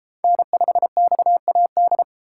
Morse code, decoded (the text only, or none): N5XAD